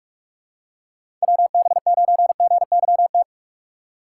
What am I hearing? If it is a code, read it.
Morse code, heard as a keyed tone, about 44 wpm, 700 Hz. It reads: WB9GYT
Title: USA Call Signs - 6 Characters - 6x Character Spacing - 45wpm